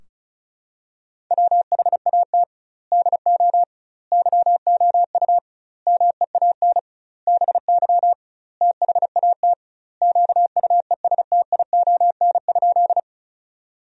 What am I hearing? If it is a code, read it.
Morse code, heard as a keyed tone, about 36 wpm, 700 Hz. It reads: WHAT DO YOU MEAN BY THAT QUESTION?
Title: 2x Word-Spacing - Sentences from Top 400 Words - 35wpm